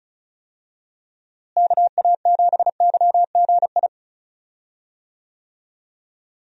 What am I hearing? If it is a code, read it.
Morse code, heard as a keyed tone, about 34 wpm, 700 Hz. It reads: KA7YGI
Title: USA Call Signs - 6 Characters - Repeated 3x - 35wpm